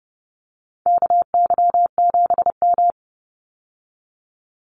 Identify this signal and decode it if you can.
Morse code, heard as a keyed tone, about 30 wpm, 700 Hz. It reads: KY7M
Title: Call Signs Repeated 3x - 30wpm